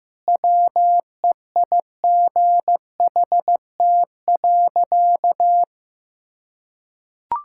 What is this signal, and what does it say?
Morse code, heard as a keyed tone, about 15 wpm, 700 Hz. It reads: WEIGHT.
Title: Sentences from Top 1000 Words - Review Part 5 - 15wpm